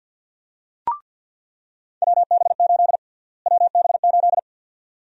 Morse code, heard as a keyed tone, about 48 wpm, 700 Hz. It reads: WB8 WB8
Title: USA Call Sign Prefixes - Repeated 2x - 50wpm